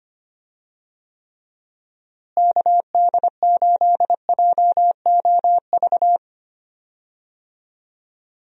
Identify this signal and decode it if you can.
Morse code, heard as a keyed tone, about 25 wpm, 700 Hz. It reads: KD8JOV
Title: USA Call Signs - 6 Characters - Repeated 3x - 25wpm